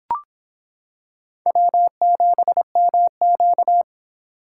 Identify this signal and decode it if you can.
Morse code, heard as a keyed tone, about 24 wpm, 700 Hz. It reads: W7MQ